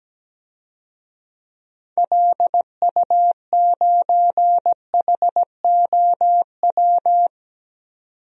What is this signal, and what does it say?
Morse code, heard as a keyed tone, about 17 wpm, 700 Hz. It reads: LU9HOW
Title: DXCC-100 Call Signs - 6 Characters - 17wpm